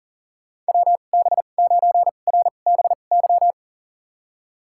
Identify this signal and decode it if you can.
Morse code, heard as a keyed tone, about 38 wpm, spaced as for 28 wpm, 700 Hz. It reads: WB9RBY
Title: USA Call Signs - 6 Characters - 2x Character Spacing - 40wpm